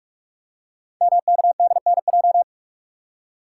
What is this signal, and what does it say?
Morse code, heard as a keyed tone, about 44 wpm, 690 Hz. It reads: MKDNJ
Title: Code Groups - 5 Letters - 45wpm